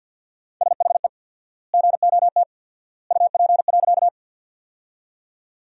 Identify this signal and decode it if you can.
Morse code, heard as a keyed tone, about 50 wpm, 700 Hz. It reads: SHE GOT UP.
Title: 4x Word-Spacing - Sentences from Top 300 Words - 50wpm